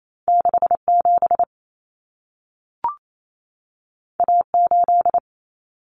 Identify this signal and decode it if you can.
Morse code, heard as a keyed tone, about 28 wpm, 700 Hz. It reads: 67 A8
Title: Common Cut Numbers - 2-Digit 28wpm